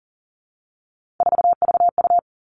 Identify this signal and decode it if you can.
Morse code, heard as a keyed tone, about 40 wpm, 700 Hz. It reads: 4VU
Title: Three Challenging Letter-Number Combinations 40wpm